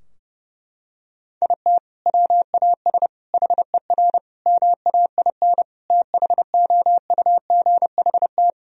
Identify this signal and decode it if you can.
Morse code, heard as a keyed tone, about 32 wpm, 700 Hz. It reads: IT WAS HER MAIN THOUGHT
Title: Sentences from Top 1000 Words - Review Part 2 - 30wpm